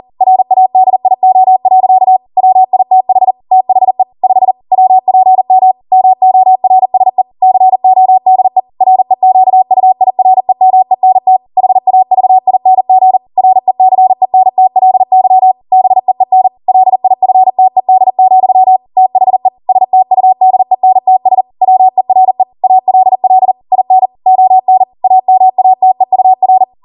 Morse code, heard as a keyed tone, about 40 wpm, 750 Hz. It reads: RADIO. WITH THE 5 WPM MORSE CODE REQUIREMENT HAVING RECENTLY BEEN LIFTED, THE STUDENTS WERE ALL IN ON AMATEUR